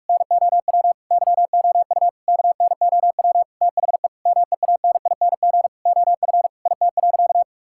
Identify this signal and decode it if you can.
Morse code, heard as a keyed tone, about 46 wpm, 700 Hz. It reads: NOW YOU KNOW THE MEANING OF IT.
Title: Sentences from Top 1000 Words - Review Part 6 - 45wpm